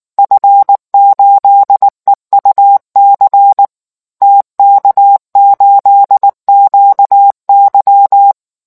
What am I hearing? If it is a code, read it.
Morse code, heard as a keyed tone, about 19 wpm, 800 Hz. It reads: F8EUC TK8QY